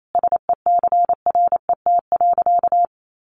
Morse code, heard as a keyed tone, about 28 wpm, 700 Hz. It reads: SECRET.
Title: Sentences from Top 1200 Words 28wpm